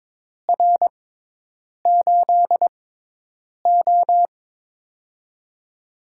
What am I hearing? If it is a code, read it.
Morse code, heard as a keyed tone, about 23 wpm, spaced as for 8 wpm, 700 Hz. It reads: R8O
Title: USA Call Signs - 4 Characters - 6x Character Spacing - 22wpm